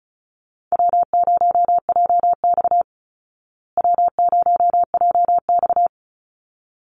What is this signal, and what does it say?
Morse code, heard as a keyed tone, about 34 wpm, 700 Hz. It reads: W0JX W0JX
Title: Call Signs Repeated 3x - 35wpm